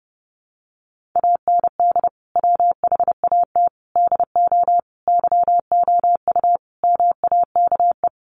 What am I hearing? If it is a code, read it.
Morse code, heard as a keyed tone, about 30 wpm, 700 Hz. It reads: AND WHAT DO YOU MAKE
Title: Sentences from Top 100 Words 30wpm